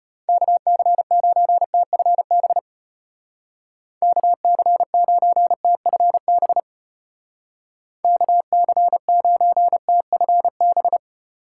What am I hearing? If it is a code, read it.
Morse code, heard as a keyed tone, about 30 wpm, 700 Hz. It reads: KC9TFB KC9TFB KC9TFB
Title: Speed Racing - USA Call Signs 25wpm